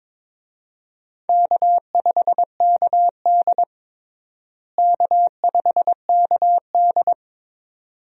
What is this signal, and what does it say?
Morse code, heard as a keyed tone, about 22 wpm, 700 Hz. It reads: K5KD K5KD
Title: USA Call Signs - 4 Characters - Repeated 2x - 22wpm